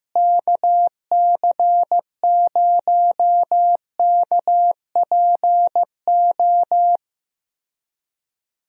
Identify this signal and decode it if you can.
Morse code, heard as a keyed tone, about 16 wpm, 700 Hz. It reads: KC0KPO
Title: USA Call Signs - 6 Characters - Repeated 3x - 15wpm